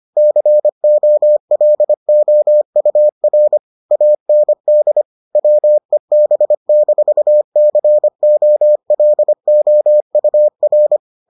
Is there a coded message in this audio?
Yes: Morse code from 0.2 to 11.0 s, heard as a keyed tone, about 25 wpm, 600 Hz. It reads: COLOUR AND WEB-COLOUR